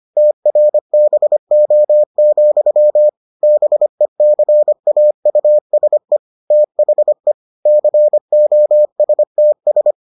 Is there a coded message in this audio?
Yes: Morse code from 0.2 to 9.9 s, heard as a keyed tone, about 25 wpm, 600 Hz. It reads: TRBO, BECAUSE THE COSTS